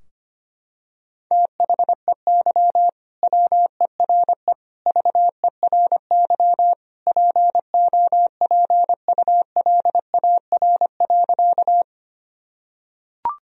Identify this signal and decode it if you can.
Morse code, heard as a keyed tone, about 25 wpm, 700 Hz. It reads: THEY WERE VERY POPULAR.
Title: Sentences from Top 1000 Words - Review Part 3 - 25wpm